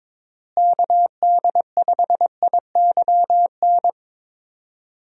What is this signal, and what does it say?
Morse code, heard as a keyed tone, about 22 wpm, 700 Hz. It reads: KD5IYN
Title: USA Call Signs - 6 Characters - Repeated 3x - 22wpm